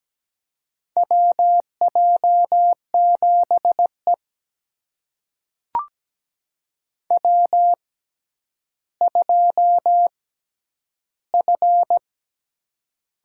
Morse code, heard as a keyed tone, about 17 wpm, spaced as for 10 wpm, 700 Hz. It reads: WJ7E W2F
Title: USA Call Signs - 4 Characters - 6x Character Spacing - 17wpm